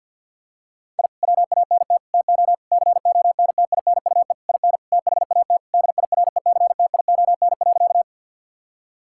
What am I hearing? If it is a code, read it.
Morse code, heard as a keyed tone, about 50 wpm, 700 Hz. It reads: I WANT TO CONTINUE IN THAT DIRECTION.
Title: Sentences from Top 1000 Words - Review Part 6 - 50wpm